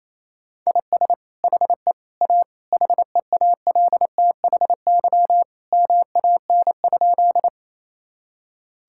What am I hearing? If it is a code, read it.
Morse code, heard as a keyed tone, about 27 wpm, 700 Hz. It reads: IS HE A HEALTHY MAN?
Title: Sentences from Top 1000 Words - Review Part 5 - 28wpm